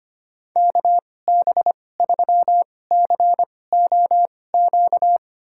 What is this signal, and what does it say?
Morse code, heard as a keyed tone, about 25 wpm, spaced as for 18 wpm, 700 Hz. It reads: KB3COQ
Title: USA Call Signs - 6 Characters - 2x Character Spacing - 25wpm